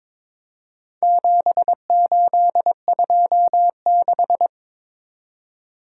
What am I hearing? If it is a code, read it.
Morse code, heard as a keyed tone, about 22 wpm, 700 Hz. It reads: 7826
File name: practice / 22wpm / Four Letter-Number Combinations - 22wpm